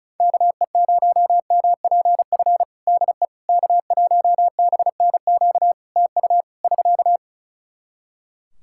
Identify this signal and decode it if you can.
Morse code, heard as a keyed tone, about 35 wpm, 700 Hz. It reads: KE0MPF DE K1BNQ TU <SK>
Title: Practice QSO - Exchange 4 - Ending - 35wpm